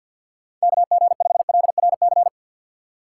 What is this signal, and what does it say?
Morse code, heard as a keyed tone, about 53 wpm, 700 Hz. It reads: KG5LRC
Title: USA Call Signs - 3x Character Spacing - 50wpm